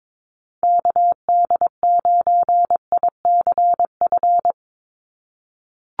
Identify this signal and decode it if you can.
Morse code, heard as a keyed tone, about 22 wpm, 700 Hz. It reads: KD9ICF
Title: USA Call Signs 22wpm